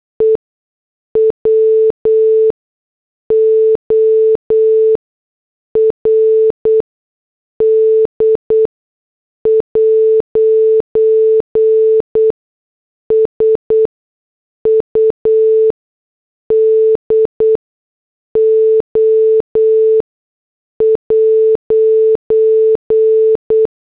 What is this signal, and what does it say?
Morse code, heard as a keyed tone, about 8 wpm, spaced as for 6 wpm, 440 Hz. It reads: E WORD'S UDO'
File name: morse / headlines